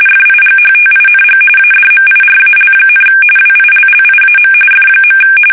RTTY, about 45.45 baud. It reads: FIGHTING ADMIRAL, WILLIAM F.
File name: fsk850 / battleshipnj